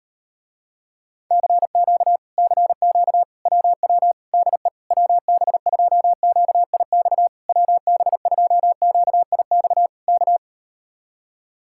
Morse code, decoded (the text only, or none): CQ CQ WW DE WB2QIX WB2QIX K